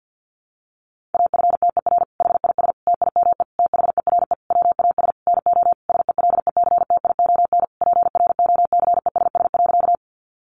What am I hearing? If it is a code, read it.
Morse code, heard as a keyed tone, about 50 wpm, 700 Hz. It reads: AFTER HIS TIME THERE WAS NO SELECTION PROCESS.
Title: Sentences from Top 1400 Words 50wpm